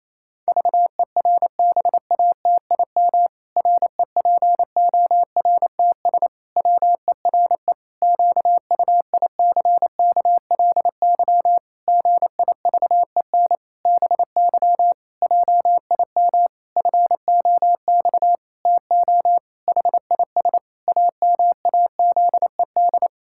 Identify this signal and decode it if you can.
Morse code, heard as a keyed tone, about 28 wpm, 700 Hz. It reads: VERBATIM REPORTS WERE QUICKLY GIVEN BY JIM FOX TO HIS AMAZED
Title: Pangram Challenge v2 - 28wpm